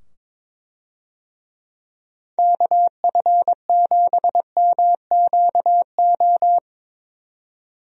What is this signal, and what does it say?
Morse code, heard as a keyed tone, about 22 wpm, 700 Hz. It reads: KF7MQO
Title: USA Call Signs - 6 Characters - 22wpm